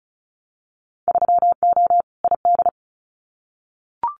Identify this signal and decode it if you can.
Morse code, heard as a keyed tone, about 36 wpm, 700 Hz. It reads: 3O ID
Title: Winter Field Day Exchanges 35wpm